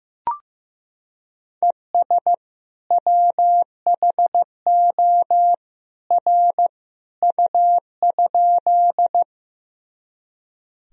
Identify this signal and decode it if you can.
Morse code, heard as a keyed tone, about 15 wpm, 700 Hz. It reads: ES WHO R U?
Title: Sentences from Top 100 Words and Abbreviations 15wpm